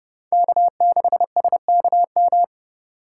Morse code, heard as a keyed tone, about 30 wpm, 700 Hz. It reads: K6SKM